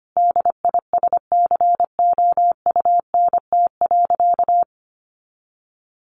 Morse code, heard as a keyed tone, about 25 wpm, 700 Hz. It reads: DISCOUNT.
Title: Sentences from Top 1200 Words 25wpm